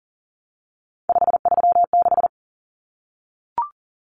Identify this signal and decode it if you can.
Morse code, heard as a keyed tone, about 40 wpm, 700 Hz. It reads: 536
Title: Numbers - 3-Digit 40wpm